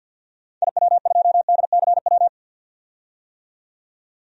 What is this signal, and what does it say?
Morse code, heard as a keyed tone, about 46 wpm, 700 Hz. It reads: IW2DCW
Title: DXCC-100 Call Signs - 6 Characters - 50wpm